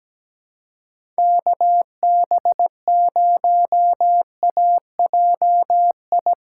Morse code, heard as a keyed tone, about 17 wpm, 700 Hz. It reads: KB0AJI
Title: USA Call Signs - 4x Character Spacing - 17wpm